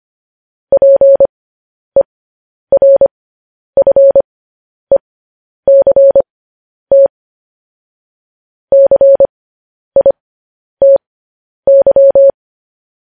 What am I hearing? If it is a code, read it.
Morse code, heard as a keyed tone, about 25 wpm, spaced as for 10 wpm, 550 Hz. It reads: PERFECT CITY